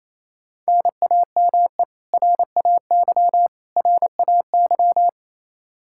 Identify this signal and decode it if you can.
Morse code, heard as a keyed tone, about 28 wpm, 700 Hz. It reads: NAME RAY RAY
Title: Practice QSO - Exchange 1 - Beginning - 28wpm